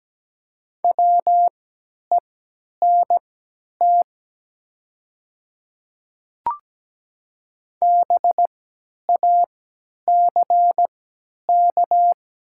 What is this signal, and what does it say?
Morse code, heard as a keyed tone, about 17 wpm, spaced as for 10 wpm, 700 Hz. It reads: WENT BACK
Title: 3x Character Spacing - Top 200 Words - 17wpm